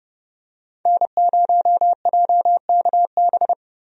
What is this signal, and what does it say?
Morse code, heard as a keyed tone, about 32 wpm, 700 Hz. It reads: N0JKB
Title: USA Call Signs - 5 Characters - Repeated 2x - 30wpm